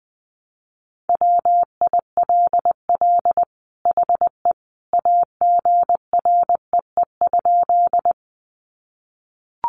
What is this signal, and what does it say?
Morse code, heard as a keyed tone, about 20 wpm, 700 Hz. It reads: WILL HE AGREE?
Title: Sentences from Top 1100 Words 20wpm